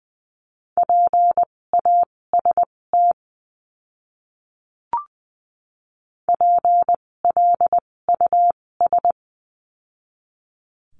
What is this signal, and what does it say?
Morse code, heard as a keyed tone, about 20 wpm, spaced as for 16 wpm, 700 Hz. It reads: PAST PLUS
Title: 256 Lesson 42a - Words 12-20wpm